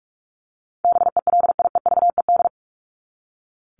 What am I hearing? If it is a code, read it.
Morse code, heard as a keyed tone, about 44 wpm, 700 Hz. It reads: BELIEVED
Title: ED Suffix - Top 100 Words with Base Words 45wpm